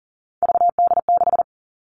Morse code, heard as a keyed tone, about 40 wpm, 700 Hz. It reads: VD6
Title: All Cut Numbers - 3-Digit 40wpm